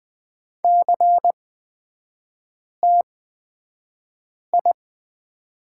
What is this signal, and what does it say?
Morse code, heard as a keyed tone, about 20 wpm, spaced as for 5 wpm, 700 Hz. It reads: CTI
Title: 224 Lesson 36e - BT - Three Characters 20wpm